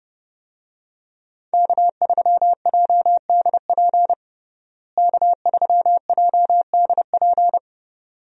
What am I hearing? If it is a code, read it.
Morse code, heard as a keyed tone, about 32 wpm, 700 Hz. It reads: K3JDP K3JDP
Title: USA Call Signs - 5 Characters - Repeated 2x - 30wpm